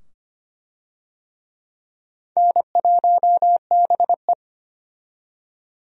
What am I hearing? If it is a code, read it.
Morse code, heard as a keyed tone, about 25 wpm, 700 Hz. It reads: N1BE
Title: USA Call Signs - 4 Characters - 25wpm